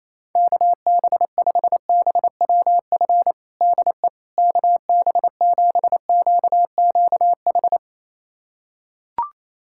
Morse code, heard as a keyed tone, about 28 wpm, 700 Hz. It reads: KB5BWF DE KB7QQH
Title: USA Callsign DE Callsign 28wpm